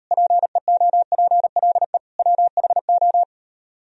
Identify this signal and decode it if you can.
Morse code, heard as a keyed tone, about 38 wpm, 700 Hz. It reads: PEOPLE WHO